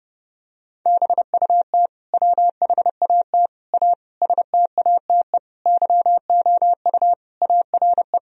Morse code, heard as a keyed tone, about 32 wpm, 700 Hz. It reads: BUT WHAT A STATE YOU ARE
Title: Sentences from Top 1000 Words - Review Part 2 - 30wpm